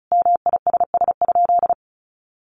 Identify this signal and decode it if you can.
Morse code, heard as a keyed tone, about 34 wpm, 700 Hz. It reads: MISS?
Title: Sentences from Top 500 Words 35wpm